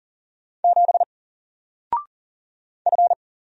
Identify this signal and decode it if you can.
Morse code, heard as a keyed tone, about 40 wpm, 700 Hz. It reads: 7 F
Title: Single Letter-Number 40wpm